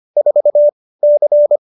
Morse code, heard as a keyed tone, about 25 wpm, 590 Hz. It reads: 4 C
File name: QSO318 25